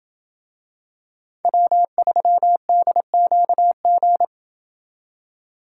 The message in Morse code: W3DQG

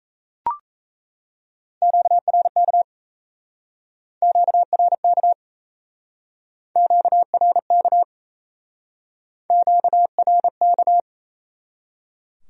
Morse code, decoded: QRK QRK QRK QRK

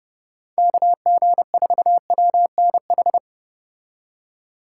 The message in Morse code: KG4WNH

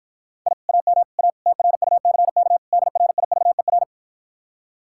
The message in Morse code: I AM A TRUCK DRIVER

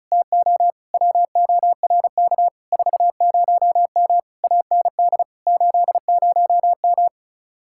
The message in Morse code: TO WORK 40M AND 80M

